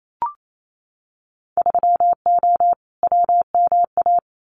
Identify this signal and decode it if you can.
Morse code, heard as a keyed tone, about 28 wpm, 700 Hz. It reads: 3O WMA
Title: Winter Field Day Exchanges Encore 28wpm